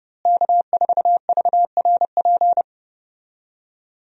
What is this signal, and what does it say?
Morse code, heard as a keyed tone, about 30 wpm, 700 Hz. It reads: K4VRP
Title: USA Call Signs - 5 Characters - Repeated 2x - 30wpm